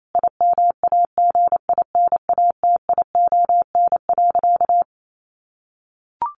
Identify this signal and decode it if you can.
Morse code, heard as a keyed tone, about 28 wpm, 700 Hz. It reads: IMAGINATION.
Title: Sentences from Top 1300 Words 28wpm